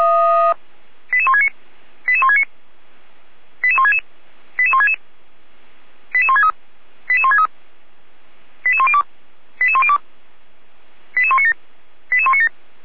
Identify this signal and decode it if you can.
Five-tone selective call (ZVEI), heard as quick runs of five tones, tones of about 70 ms. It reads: 88169 88169 88166 88166 88163 88163 88102 88102 88197 88197